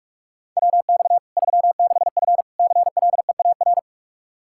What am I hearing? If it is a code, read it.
Morse code, heard as a keyed tone, about 46 wpm, 700 Hz. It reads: WX 36F CLEAR
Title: Practice QSO - Exchange 2 - Info - 45wpm